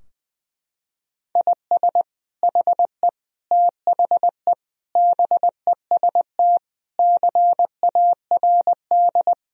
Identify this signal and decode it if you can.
Morse code, heard as a keyed tone, about 20 wpm, 700 Hz. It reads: IS HE THE BEST CARD
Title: Sentences from Top 1000 Words - Review Part 4 - 20wpm